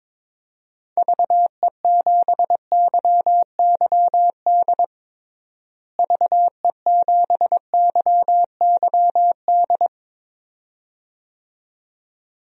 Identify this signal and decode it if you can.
Morse code, heard as a keyed tone, about 22 wpm, 700 Hz. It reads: VE7YYD VE7YYD
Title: DXCC-100 Call Signs - 6 Characters - Repeated 2x - 22wpm